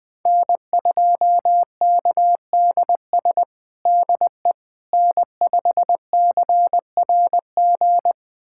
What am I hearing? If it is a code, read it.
Morse code, heard as a keyed tone, about 19 wpm, 700 Hz. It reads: N2KDS DE N5CRG
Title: USA Callsign DE Callsign - 5 Characters - 20wpm